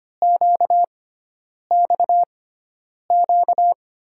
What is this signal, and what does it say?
Morse code, heard as a keyed tone, about 25 wpm, spaced as for 9 wpm, 700 Hz. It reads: QXQ
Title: USA Call Signs - 6 Characters - 6x Character Spacing - 25wpm